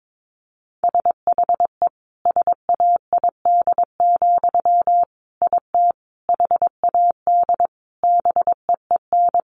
Morse code, heard as a keyed tone, about 22 wpm, 700 Hz. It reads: SHE SAID, IT HAD BEEN
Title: Sentences from Top 800 Words 22wpm